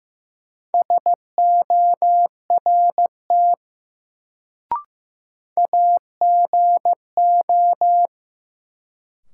Morse code, heard as a keyed tone, about 16 wpm, 700 Hz. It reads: SORT AGO
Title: Sets of 2 Words - New Word Drill - 15wpm